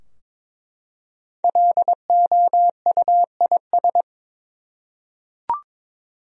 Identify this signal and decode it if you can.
Morse code, heard as a keyed tone, about 22 wpm, 700 Hz. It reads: LOUIS
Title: USA Common Names - Max 5 Letters - 22wpm